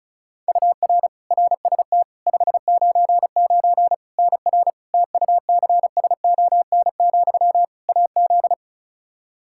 Morse code, heard as a keyed tone, about 36 wpm, 700 Hz. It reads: UR RST 599 NR TUCSON, AZ
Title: Practice QSO - Exchange 1 - Beginning - 35wpm